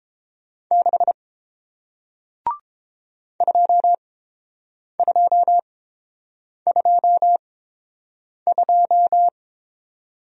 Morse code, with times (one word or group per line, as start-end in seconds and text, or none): 0.7-1.1 6
3.4-9.3 2222